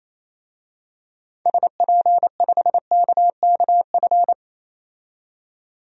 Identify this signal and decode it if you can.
Morse code, heard as a keyed tone, about 28 wpm, 700 Hz. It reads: SP5KKF
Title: DXCC-100 Call Signs - 6 Characters - 28wpm